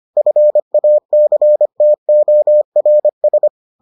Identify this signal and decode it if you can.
Morse code, heard as a keyed tone, about 25 wpm, 600 Hz. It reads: FACTORS